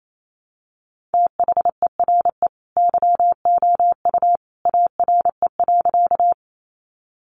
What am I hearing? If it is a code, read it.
Morse code, heard as a keyed tone, about 28 wpm, 700 Hz. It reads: THERE YOU ARE.
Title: Sentences from Top 100 Words 28wpm